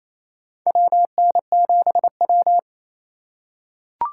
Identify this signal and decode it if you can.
Morse code, heard as a keyed tone, about 28 wpm, 700 Hz. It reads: WN7W
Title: USA Call Signs - 4 Characters - 2x Character Spacing - 28wpm